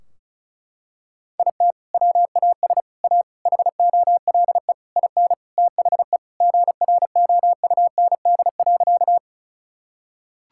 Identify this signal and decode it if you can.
Morse code, heard as a keyed tone, about 36 wpm, 700 Hz. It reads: IT WAS A HOLE IN THE GROUND.
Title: Sentences from Top 1600 Words 35wpm